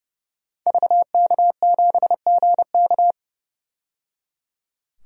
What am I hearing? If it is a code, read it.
Morse code, heard as a keyed tone, about 30 wpm, 700 Hz. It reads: VK7GK
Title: DXCC-100 Call Signs - 5 Characters - 30wpm